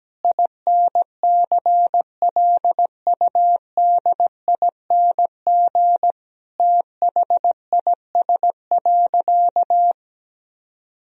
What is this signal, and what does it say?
Morse code, heard as a keyed tone, about 17 wpm, 700 Hz. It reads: INCLUDING THIS.
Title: Sentences from Top 1000 Words - Review Part 4 - 17wpm